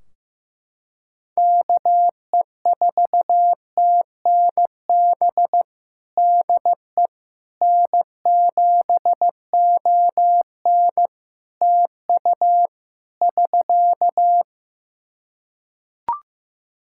Morse code, heard as a keyed tone, about 15 wpm, 700 Hz. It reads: KE4TNB DE N7ON TU <SK>
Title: Practice QSO - Exchange 4 - Ending - 15wpm